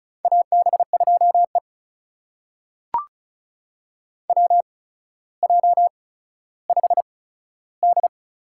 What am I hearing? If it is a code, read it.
Morse code, heard as a keyed tone, about 36 wpm, spaced as for 15 wpm, 700 Hz. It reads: AB2E WJ5D
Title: USA Call Signs - 4 Characters - 8x Character Spacing - 35wpm